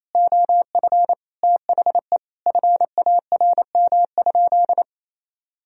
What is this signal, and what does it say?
Morse code, heard as a keyed tone, about 27 wpm, 700 Hz. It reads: OF THE FARM?
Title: Sentences from Top 1000 Words - Review Part 3 - 28wpm